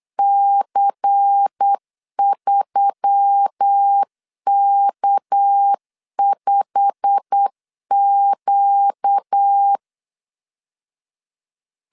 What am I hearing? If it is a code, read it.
Morse code, heard as a keyed tone, about 8 wpm, 800 Hz. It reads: C3K5Q
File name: LEZIONI / 9 2 8wpm 3437